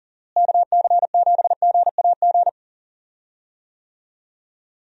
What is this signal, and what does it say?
Morse code, heard as a keyed tone, about 40 wpm, 700 Hz. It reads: KC7GAG
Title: USA Call Signs - 6 Characters - Repeated 2x - 40wpm